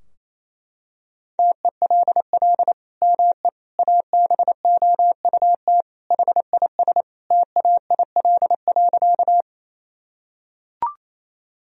Morse code, heard as a keyed tone, about 28 wpm, 700 Hz. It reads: TELL ME ABOUT HIS TAIL.
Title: Sentences from Top 1000 Words - Review Part 3 - 28wpm